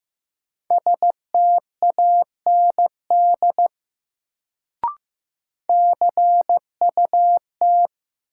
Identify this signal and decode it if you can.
Morse code, heard as a keyed tone, about 15 wpm, 700 Hz. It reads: STAND CUT
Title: Sentences from Top 300 Words - New Word Drill - 15wpm